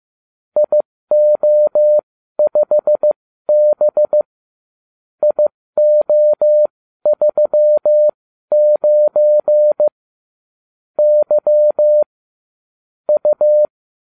Morse code, heard as a keyed tone, about 15 wpm, spaced as for 12 wpm, 600 Hz. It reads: IO5B IO39 Y U